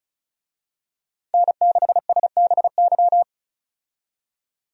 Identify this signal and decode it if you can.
Morse code, heard as a keyed tone, about 36 wpm, 700 Hz. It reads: N6SBY
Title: USA Call Signs - 5 Characters - 35wpm